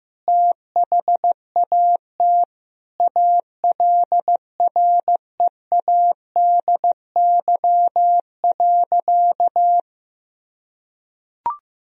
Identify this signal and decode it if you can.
Morse code, heard as a keyed tone, about 15 wpm, 700 Hz. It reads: THAT ALREADY.
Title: Sentences from Top 1000 Words - Review Part 3 - 15wpm